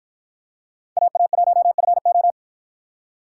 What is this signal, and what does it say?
Morse code, heard as a keyed tone, about 53 wpm, 700 Hz. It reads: AA1FO